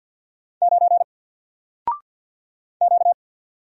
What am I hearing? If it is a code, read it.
Morse code, heard as a keyed tone, about 50 wpm, 700 Hz. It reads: 9 Q
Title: Single Letter-Number 50wpm